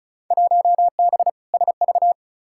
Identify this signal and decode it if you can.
Morse code, heard as a keyed tone, about 34 wpm, 700 Hz. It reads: 1B SV
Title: Field Day Exchanges Encore - 35wpm